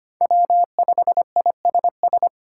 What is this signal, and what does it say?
Morse code, heard as a keyed tone, about 25 wpm, 700 Hz. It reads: W5ISS
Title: USA Call Signs - 8x Character Spacing - 25wpm